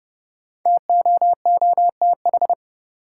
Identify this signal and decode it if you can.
Morse code, heard as a keyed tone, about 29 wpm, 700 Hz. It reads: TOOTH
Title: Five Letter Words - 30wpm